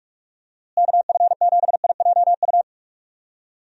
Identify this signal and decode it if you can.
Morse code, heard as a keyed tone, about 43 wpm, 700 Hz. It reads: KF7IJU
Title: USA Call Signs - 6 Characters - Repeated 3x - 45wpm